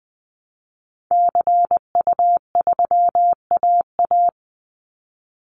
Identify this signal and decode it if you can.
Morse code, heard as a keyed tone, about 20 wpm, 700 Hz. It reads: CU3AA
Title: Call Signs Repeated 3x - 20wpm